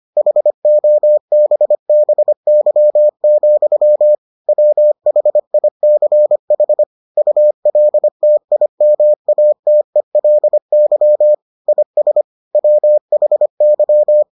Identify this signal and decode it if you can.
Morse code, heard as a keyed tone, about 25 wpm, 600 Hz. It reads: HOBBY, WHICH ULTIMATELY IS WHY